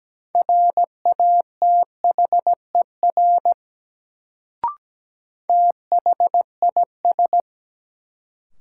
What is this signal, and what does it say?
Morse code, heard as a keyed tone, about 17 wpm, 700 Hz. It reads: RATHER THIS
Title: Sets of 2 Words - New Word Drill - 17wpm